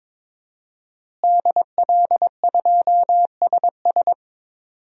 Morse code, heard as a keyed tone, about 22 wpm, 700 Hz. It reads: DL2SS
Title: DXCC-100 Call Signs - 5 Characters - Repeated 3x - 22wpm